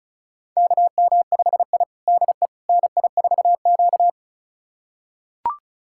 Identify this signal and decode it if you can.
Morse code, heard as a keyed tone, about 36 wpm, 700 Hz. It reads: KM5I DE NI4Q
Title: USA Callsign DE Callsign - 4 Characters - 35wpm